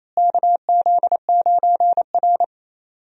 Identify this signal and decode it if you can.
Morse code, heard as a keyed tone, about 28 wpm, 700 Hz. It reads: KZ9R
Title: USA Call Signs - 4 Characters - Repeated 3x - 28wpm